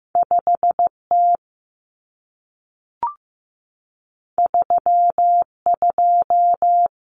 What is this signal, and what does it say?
Morse code, heard as a keyed tone, about 15 wpm, 700 Hz. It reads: 5T 32
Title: Common Cut Numbers - 2-Digit 15wpm